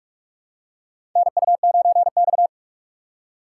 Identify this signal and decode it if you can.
Morse code, heard as a keyed tone, about 46 wpm, 700 Hz. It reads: NU9X
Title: USA Call Signs - 4 Characters - 6x Character Spacing - 45wpm